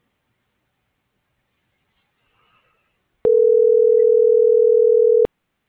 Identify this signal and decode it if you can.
Ringback tone (North America), 440 + 480 Hz.